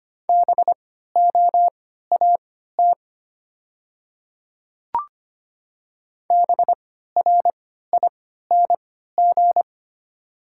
Morse code, heard as a keyed tone, about 25 wpm, spaced as for 14 wpm, 700 Hz. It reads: BOAT BRING
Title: 3x Character Spacing - Top 500 Words - 25wpm